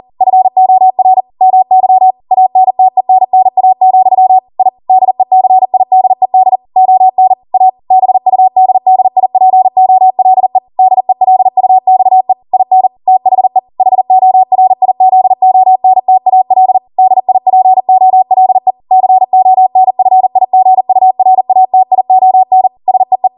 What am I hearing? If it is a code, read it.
Morse code, heard as a keyed tone, about 40 wpm, 750 Hz. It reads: FOR MY ANTENNA, I DECIDED ON A BUDDIPOLE DELUXE IN THE HORIZONTAL DIPOLE CONFIGURATION SEE